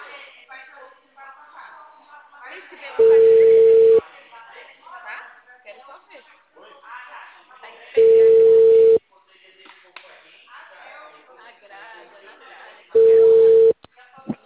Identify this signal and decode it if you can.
Ringback tone (China, 450 Hz), with speech.